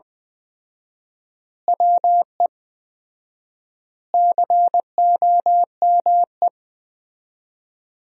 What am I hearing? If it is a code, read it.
Morse code, heard as a keyed tone, about 20 wpm, 700 Hz. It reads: WE COME
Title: 4x Word-Spacing - Sentences from Top 100 Words - 20wpm